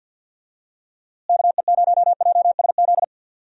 Morse code, heard as a keyed tone, about 48 wpm, 700 Hz. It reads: KE0JSZ
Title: USA Call Signs - 6 Characters - Repeated 2x - 50wpm